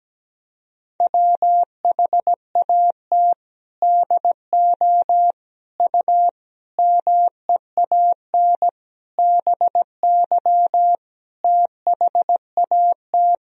Morse code, heard as a keyed tone, about 17 wpm, 700 Hz. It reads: WHAT DO U MEAN BY THAT
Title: Sentences from Top 400 Words and Abbreviations 17wpm